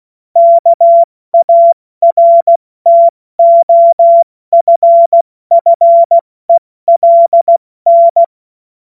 Morse code, heard as a keyed tone, about 16 wpm, spaced as for 14 wpm, 680 Hz. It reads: KARTOFFELN
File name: Morsekurs / Witze / Witz 18. Mai 2021, 70 F80